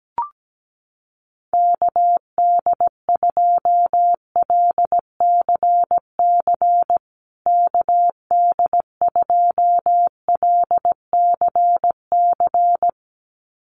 Morse code, heard as a keyed tone, about 17 wpm, 700 Hz. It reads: KD2LCC KD2LCC
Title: USA Call Signs Repeated 2x 17wpm